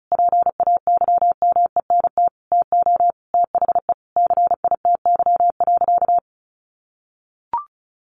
Morse code, heard as a keyed tone, about 34 wpm, 700 Hz. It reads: PAYMENT TO THE CITY.
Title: Sentences from Top 1400 Words 35wpm